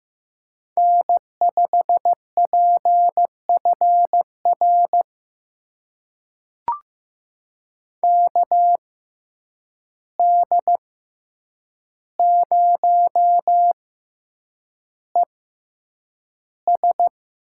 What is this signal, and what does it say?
Morse code, heard as a keyed tone, about 15 wpm, spaced as for 8 wpm, 700 Hz. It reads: N5PFR KD0ES
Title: USA Call Signs - 5 Characters - 6x Character Spacing - 15wpm